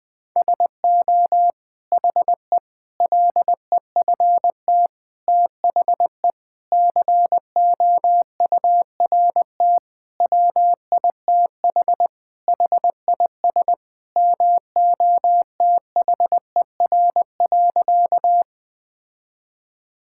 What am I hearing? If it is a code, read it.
Morse code, heard as a keyed tone, about 20 wpm, 700 Hz. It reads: SO HE LEFT THE COURT WITH HIS MOTHER.
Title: Sentences from Top 1600 Words 20wpm